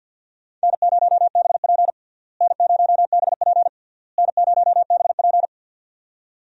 Morse code, heard as a keyed tone, about 52 wpm, 700 Hz. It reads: N0BP N0BP N0BP
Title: USA Call Signs - 4 Characters - Repeated 3x - 50wpm